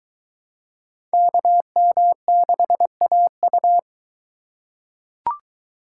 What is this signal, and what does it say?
Morse code, heard as a keyed tone, about 22 wpm, 700 Hz. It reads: KM6AU